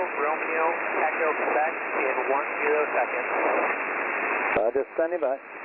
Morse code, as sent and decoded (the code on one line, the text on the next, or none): none